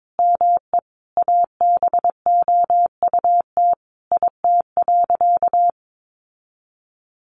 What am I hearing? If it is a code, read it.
Morse code, heard as a keyed tone, about 22 wpm, 700 Hz. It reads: ME ABOUT IT.
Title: Sentences from Top 1200 Words 22wpm